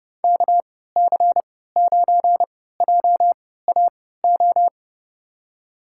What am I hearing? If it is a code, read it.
Morse code, heard as a keyed tone, about 30 wpm, spaced as for 17 wpm, 700 Hz. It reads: KC9JAO
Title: USA Call Signs - 6 Characters - 3x Character Spacing - 30wpm